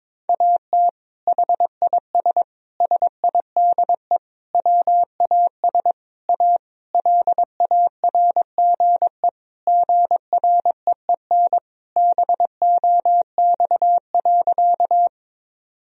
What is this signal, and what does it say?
Morse code, heard as a keyed tone, about 22 wpm, 700 Hz. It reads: AT HIS SIDE WAS A LARGE GREEN BOX.
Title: Sentences from Top 1000 Words - Review Part 2 - 22wpm